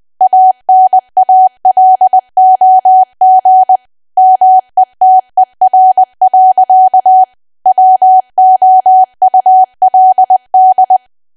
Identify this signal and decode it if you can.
Morse code, heard as a keyed tone, about 20 wpm, 750 Hz. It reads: ANALOG METER. WOULD
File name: cwa / W1AW-Level3 / 20WPM